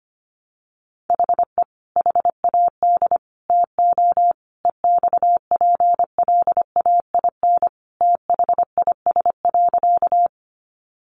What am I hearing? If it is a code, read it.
Morse code, heard as a keyed tone, about 25 wpm, 700 Hz. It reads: HE HAD TO EXPLAIN THIS.
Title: Sentences from Top 1100 Words 25wpm